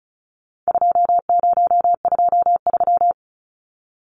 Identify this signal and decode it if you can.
Morse code, heard as a keyed tone, about 34 wpm, 700 Hz. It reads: 2023
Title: Numbers - 4-Digit 35wpm